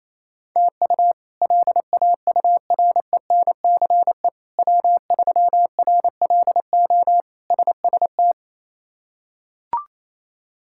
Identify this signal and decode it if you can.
Morse code, heard as a keyed tone, about 28 wpm, 700 Hz. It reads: TU LAURENCE W3RLO SST
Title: K1USN SST QSO Practice - 28wpm